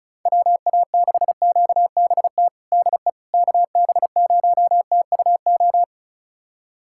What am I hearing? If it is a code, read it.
Morse code, heard as a keyed tone, about 36 wpm, 700 Hz. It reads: WA6QBT DE KB0TUO
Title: USA Callsign DE Callsign 35wpm